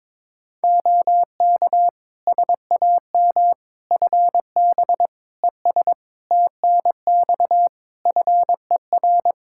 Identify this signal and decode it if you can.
Morse code, heard as a keyed tone, about 22 wpm, 700 Hz. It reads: OK SAM FB ES TNX FER
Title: Practice QSO - Exchange 2 - Info - 22wpm